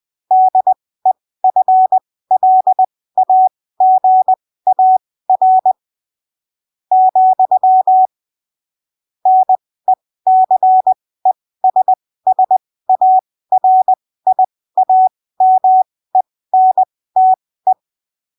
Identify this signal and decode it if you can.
Morse code, heard as a keyed tone, about 20 wpm, spaced as for 14 wpm, 750 Hz. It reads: DEFLAGAR, NECESSARIAMENTE
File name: arquivos / texto / texto 30 20ppm